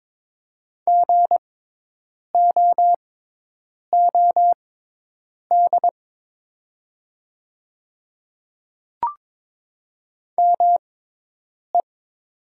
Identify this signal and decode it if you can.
Morse code, heard as a keyed tone, about 22 wpm, spaced as for 8 wpm, 700 Hz. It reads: GOOD ME
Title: 6x Character Spacing - Top 200 Words - 22wpm